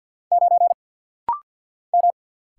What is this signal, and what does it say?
Morse code, heard as a keyed tone, about 50 wpm, 700 Hz. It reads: 9 M
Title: Single Letter-Number Alt - Rapid-Fire 50wpm